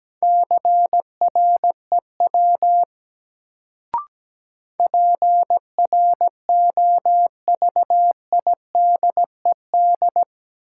Morse code, heard as a keyed tone, about 17 wpm, 700 Hz. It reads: CREW PROVIDED
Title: Sentences from Top 1500 Words - New Word Drill Alt - 17wpm